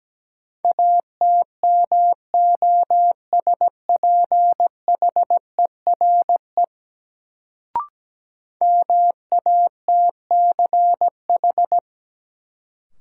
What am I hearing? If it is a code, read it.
Morse code, heard as a keyed tone, about 17 wpm, 700 Hz. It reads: ATMOSPHERE MATCH
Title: Sentences from Top 1600 Words - New Word Drill - 17wpm